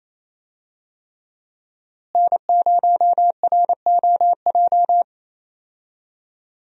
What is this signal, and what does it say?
Morse code, heard as a keyed tone, about 28 wpm, 700 Hz. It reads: N0ROJ
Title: USA Call Signs - 5 Characters - Repeated 3x - 28wpm